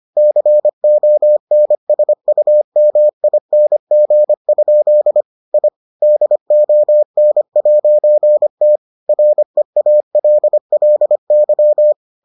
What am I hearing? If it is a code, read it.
Morse code, heard as a keyed tone, about 25 wpm, 600 Hz. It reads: CONSUMING? I DON'T REALLY